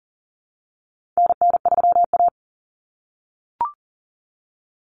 Morse code, heard as a keyed tone, about 40 wpm, 700 Hz. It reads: NN3A
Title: Common Cut Numbers - 4-Digit 40wpm